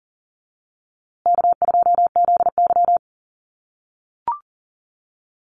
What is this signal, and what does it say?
Morse code, heard as a keyed tone, about 40 wpm, 700 Hz. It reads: K2ZY
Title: USA Call Signs Repeated 2x 40wpm